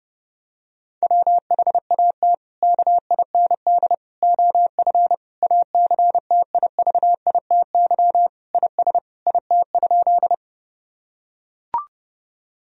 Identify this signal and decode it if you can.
Morse code, heard as a keyed tone, about 30 wpm, 700 Hz. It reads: WHAT KIND OF ACTIVITY IS IT?
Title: Sentences from Top 1000 Words - Review Part 4 - 30wpm